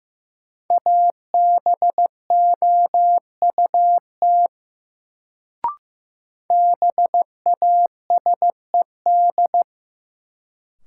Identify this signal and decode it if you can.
Morse code, heard as a keyed tone, about 15 wpm, 700 Hz. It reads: ABOUT BASED
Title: Sets of 2 Words - New Word Drill - 15wpm